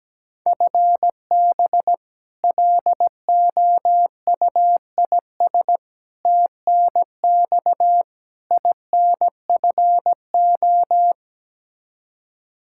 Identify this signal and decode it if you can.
Morse code, heard as a keyed tone, about 17 wpm, 700 Hz. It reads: FB LOUIS TNX INFO
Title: Practice QSO - Exchange 3 - More Info - 17wpm